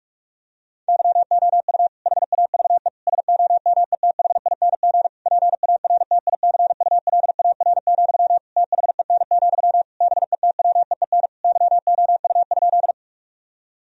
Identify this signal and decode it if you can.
Morse code, heard as a keyed tone, about 46 wpm, 700 Hz. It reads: YOU HAVE SOMETHING PARTICULAR, THEN, BETWEEN YOU?
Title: Sentences from Top 1000 Words - Review Part 4 - 45wpm